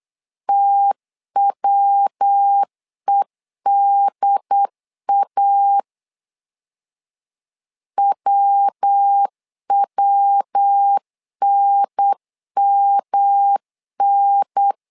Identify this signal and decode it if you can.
Morse code, heard as a keyed tone, about 8 wpm, 800 Hz. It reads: TWEDA WWNMN
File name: LEZIONI / 3 MWRO / 3 MW 8wpm 2501